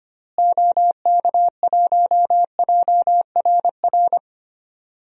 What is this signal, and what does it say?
Morse code, heard as a keyed tone, about 25 wpm, 700 Hz. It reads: OK1JRR
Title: DXCC-100 Call Signs - 6 Characters - Repeated 3x - 25wpm